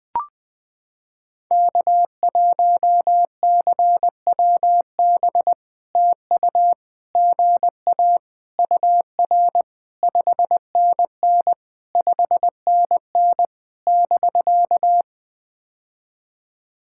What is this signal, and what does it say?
Morse code, heard as a keyed tone, about 20 wpm, 700 Hz. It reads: K1CWB TU GA UR 5NN 5NN <BK>
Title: POTA QSO Practice - 20wpm